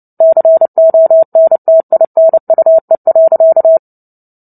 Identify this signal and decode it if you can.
Morse code, heard as a keyed tone, about 30 wpm, 650 Hz. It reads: CONTINUE.
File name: Book Two / 03-Chapter 3 - 29 WPM